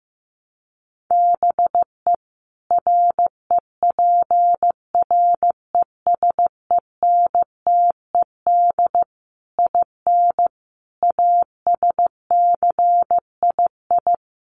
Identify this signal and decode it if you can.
Morse code, heard as a keyed tone, about 15 wpm, 700 Hz. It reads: BE REPRESENTED IN ASCII
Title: Sets of 5 Words 15wpm